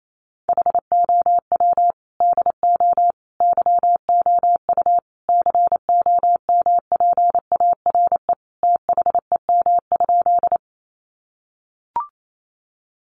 Morse code, heard as a keyed tone, about 28 wpm, 700 Hz. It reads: HOW DO YOU COMPARE THEM?
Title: Sentences from Top 1300 Words 28wpm